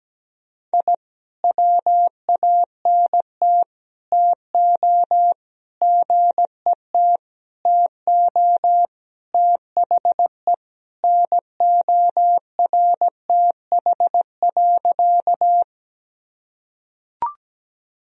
Morse code, heard as a keyed tone, about 17 wpm, 700 Hz. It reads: I WANT TO GET TO THE NORTH.
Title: Sentences from Top 1000 Words - Review Part 1 - 17wpm